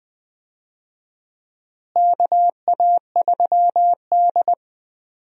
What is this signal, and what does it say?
Morse code, heard as a keyed tone, about 20 wpm, 700 Hz. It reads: KA3D
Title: USA Call Signs - 4 Characters - Repeated 2x - 20wpm